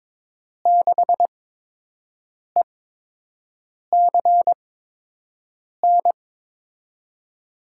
Morse code, heard as a keyed tone, about 22 wpm, spaced as for 6 wpm, 700 Hz. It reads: 6ECN
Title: USA Call Signs - 6 Characters - 8x Character Spacing - 22wpm